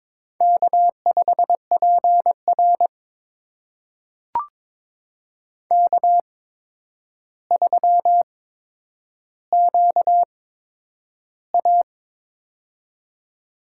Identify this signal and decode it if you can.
Morse code, heard as a keyed tone, about 22 wpm, spaced as for 9 wpm, 700 Hz. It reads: K5PR K3QA
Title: USA Call Signs - 4 Characters - 8x Character Spacing - 22wpm